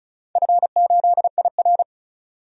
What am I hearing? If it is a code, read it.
Morse code, heard as a keyed tone, about 34 wpm, 700 Hz. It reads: F8IR
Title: DXCC-100 Call Signs - 4 Characters - 35wpm